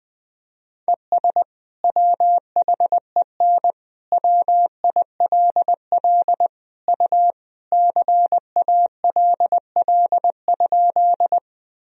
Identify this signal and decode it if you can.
Morse code, heard as a keyed tone, about 20 wpm, 700 Hz. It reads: ES WHEN WILL U CALL?
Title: Sentences from Top 100 Words and Abbreviations 20wpm